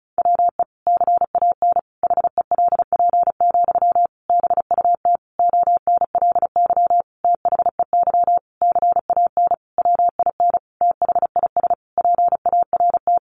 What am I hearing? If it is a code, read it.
Morse code, heard as a keyed tone, about 34 wpm, 700 Hz. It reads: WE CAN HELP, BUT ONLY THEY CAN WIN THIS PART
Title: Sentences from Top 1500 Words 35wpm